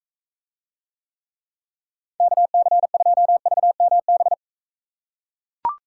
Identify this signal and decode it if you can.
Morse code, heard as a keyed tone, about 40 wpm, 700 Hz. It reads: KC2VMB